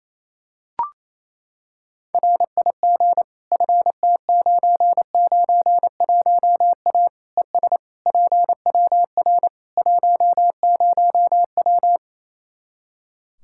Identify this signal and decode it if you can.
Morse code, heard as a keyed tone, about 28 wpm, 700 Hz. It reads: RIG FT991A ES PWR 10W
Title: Practice QSO - Exchange 2 - Info - 28wpm